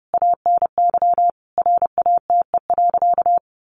Morse code, heard as a keyed tone, about 30 wpm, 700 Hz. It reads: ANY RATE.
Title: Sentences from Top 900 Words 30wpm